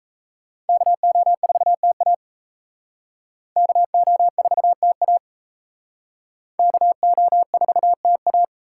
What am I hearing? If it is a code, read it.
Morse code, heard as a keyed tone, about 38 wpm, 700 Hz. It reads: KO4TA KO4TA KO4TA